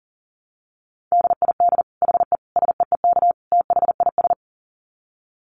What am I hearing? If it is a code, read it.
Morse code, heard as a keyed tone, about 40 wpm, 700 Hz. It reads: DID HE SEEK THIS
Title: Dit Challenge 40wpm